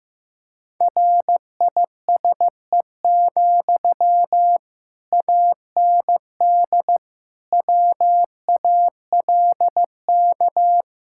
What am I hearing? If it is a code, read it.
Morse code, heard as a keyed tone, about 16 wpm, 700 Hz. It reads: RISE, AND WALK
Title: Sentences from Top 1000 Words - Review Part 6 - 15wpm